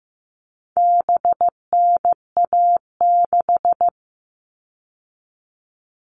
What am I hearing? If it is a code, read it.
Morse code, heard as a keyed tone, about 15 wpm, 700 Hz. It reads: BNA6
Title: All Cut Numbers - 4-Digit 15wpm